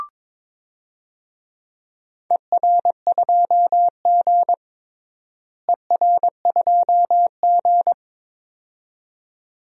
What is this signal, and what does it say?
Morse code, heard as a keyed tone, about 22 wpm, 700 Hz. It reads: ER2G ER2G
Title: DXCC-100 Call Signs - 4 Characters - Repeated 2x - 22wpm